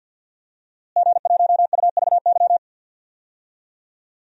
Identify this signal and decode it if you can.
Morse code, heard as a keyed tone, about 51 wpm, 700 Hz. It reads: G1UVY
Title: DXCC-100 Call Signs - 5 Characters - Repeated 2x - 50wpm